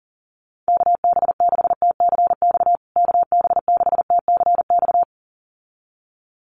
Morse code, heard as a keyed tone, about 40 wpm, 700 Hz. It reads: KB6TCX KB6TCX
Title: USA Call Signs Repeated 2x 40wpm